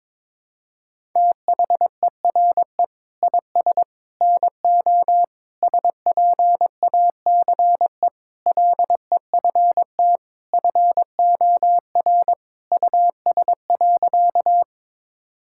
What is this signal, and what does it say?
Morse code, heard as a keyed tone, about 22 wpm, 700 Hz. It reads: THERE IS NO SPACE LEFT FOR US.
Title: Sentences from Top 1000 Words - Review Part 5 - 22wpm